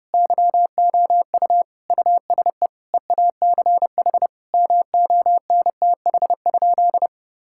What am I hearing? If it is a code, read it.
Morse code, heard as a keyed tone, about 30 wpm, 700 Hz. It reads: YOU USE EACH MONTH?
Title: Sentences from Top 1000 Words - Review Part 6 - 30wpm